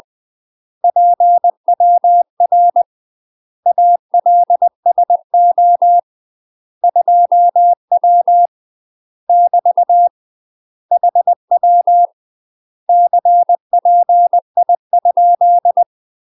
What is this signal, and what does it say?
Morse code, heard as a keyed tone, about 20 wpm, 700 Hz. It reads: PWR ALSO 2W <BT> HW CPI?
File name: UK MASTER QSOs TRAINING / 20WPM / UK MASTER QSOs 0020